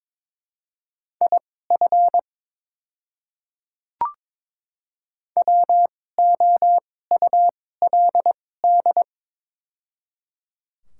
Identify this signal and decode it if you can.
Morse code, heard as a keyed tone, about 22 wpm, spaced as for 16 wpm, 700 Hz. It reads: IF WOULD
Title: 2x Character Spacing - Top 200 Words - 22wpm